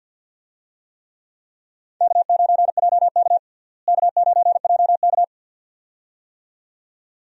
Morse code, heard as a keyed tone, about 52 wpm, 700 Hz. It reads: K9JK K9JK
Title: USA Call Signs - 4 Characters - Repeated 2x - 50wpm